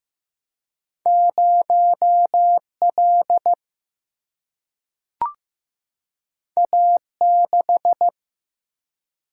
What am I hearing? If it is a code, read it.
Morse code, heard as a keyed tone, about 15 wpm, 700 Hz. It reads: 0 L A 6
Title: Single-Letter Single-Number 15wpm